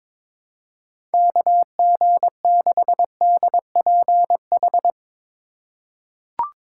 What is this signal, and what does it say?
Morse code, heard as a keyed tone, about 22 wpm, 700 Hz. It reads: KG6DPH
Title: USA Call Signs - 6 Characters - 3x Character Spacing - 22wpm